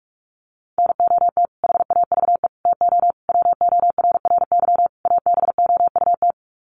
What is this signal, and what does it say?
Morse code, heard as a keyed tone, about 44 wpm, 700 Hz. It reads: NOT HAVE TO WORRY ABOUT
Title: Sets of 5 Words 45wpm